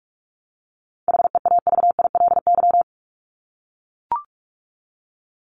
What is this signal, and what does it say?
Morse code, heard as a keyed tone, about 44 wpm, 700 Hz. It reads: HEAVILY
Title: LY Suffix - Top 100 Words with Base Words 45wpm